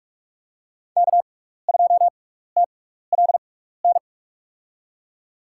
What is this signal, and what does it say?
Morse code, heard as a keyed tone, about 47 wpm, spaced as for 16 wpm, 700 Hz. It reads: K2TLN
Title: USA Call Signs - 5 Characters - 6x Character Spacing - 45wpm